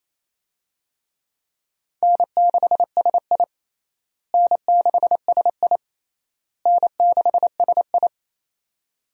Morse code, heard as a keyed tone, about 27 wpm, 700 Hz. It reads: N6SI N6SI N6SI
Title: USA Call Signs - 4 Characters - Repeated 3x - 28wpm